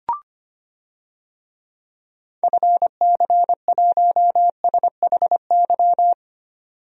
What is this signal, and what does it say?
Morse code, heard as a keyed tone, about 25 wpm, 700 Hz. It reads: FC1SHY
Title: DXCC-100 Call Signs - 6 Characters - 25wpm